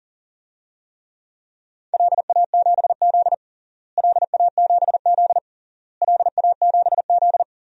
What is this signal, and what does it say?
Morse code, heard as a keyed tone, about 37 wpm, 700 Hz. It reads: LA7Z LA7Z LA7Z
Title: DXCC-100 Call Signs - 4 Characters - Repeated 3x - 40wpm